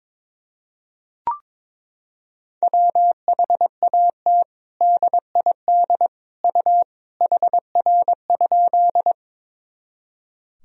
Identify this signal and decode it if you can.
Morse code, heard as a keyed tone, about 22 wpm, 700 Hz. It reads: WHAT DID U HR?
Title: Sentences from Top 300 Words and Abbreviations 22wpm